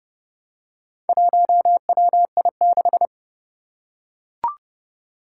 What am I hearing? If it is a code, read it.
Morse code, heard as a keyed tone, about 30 wpm, 700 Hz. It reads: 1WI6
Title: Four Challenging Letter-Number Combinations 30wpm